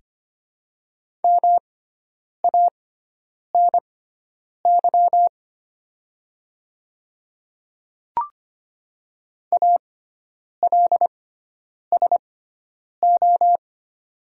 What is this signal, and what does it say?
Morse code, heard as a keyed tone, about 25 wpm, spaced as for 9 wpm, 700 Hz. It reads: MANY ALSO
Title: 6x Character Spacing - Top 300 Words - 25wpm